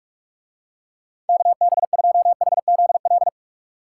Morse code, heard as a keyed tone, about 45 wpm, 700 Hz. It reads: KB2HZL